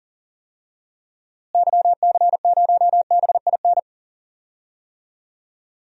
Morse code, heard as a keyed tone, about 40 wpm, 700 Hz. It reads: YC0BIN
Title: DXCC-100 Call Signs - 6 Characters - Repeated 3x - 40wpm